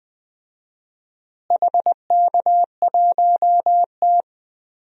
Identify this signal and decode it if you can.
Morse code, heard as a keyed tone, about 21 wpm, 700 Hz. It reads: HK1T
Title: DXCC-100 Call Signs - 4 Characters - Repeated 3x - 20wpm